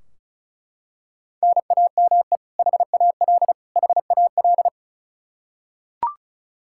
Morse code, heard as a keyed tone, about 36 wpm, 700 Hz. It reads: NAME HAL HAL
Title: Practice QSO - Exchange 1 - Beginning - 35wpm